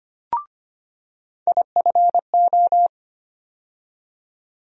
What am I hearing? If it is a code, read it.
Morse code, heard as a keyed tone, about 25 wpm, 700 Hz. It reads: IFO
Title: USA Call Sign Postfixes - 3 Letters 25wpm